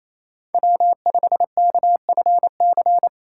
Morse code, heard as a keyed tone, about 33 wpm, 700 Hz. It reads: W5KFC
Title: Speed Racing - USA 1x3 Call Signs 28wpm